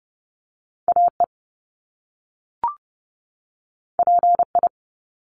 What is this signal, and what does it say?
Morse code, heard as a keyed tone, about 30 wpm, 700 Hz. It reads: AE PI
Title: Top Two Letter Combinations in Words 30wpm